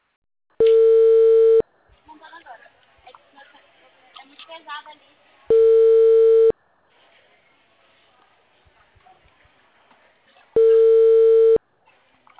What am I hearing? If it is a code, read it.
Ringback tone (China, 450 Hz), with speech.
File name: out- -2006- - - .6